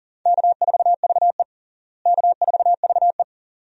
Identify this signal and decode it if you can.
Morse code, heard as a keyed tone, about 40 wpm, 700 Hz. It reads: K4VE K4VE
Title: USA Call Signs - 4 Characters - Repeated 2x - 40wpm